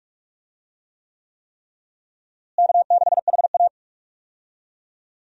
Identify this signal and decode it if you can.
Morse code, heard as a keyed tone, about 46 wpm, 700 Hz. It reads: K6HA
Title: USA Call Signs - 4 Characters - Repeated 3x - 45wpm